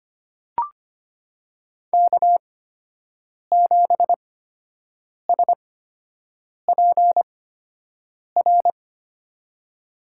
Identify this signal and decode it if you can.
Morse code, heard as a keyed tone, about 25 wpm, spaced as for 7 wpm, 700 Hz. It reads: K7SPR
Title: USA Call Signs - 5 Characters - 8x Character Spacing - 25wpm